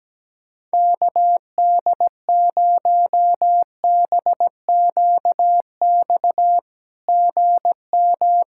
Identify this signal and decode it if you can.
Morse code, heard as a keyed tone, about 17 wpm, 700 Hz. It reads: KD0BQX GM
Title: SOTA QSO Practice 17wpm